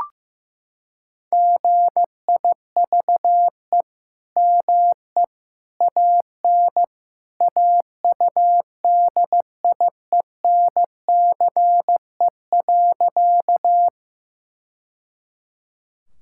Morse code, heard as a keyed tone, about 16 wpm, 700 Hz. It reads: GIVE ME AN AUDIENCE.
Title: Sentences from Top 1000 Words - Review Part 5 - 15wpm